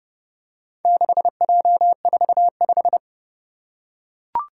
Morse code, heard as a keyed tone, about 28 wpm, 700 Hz. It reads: 6J45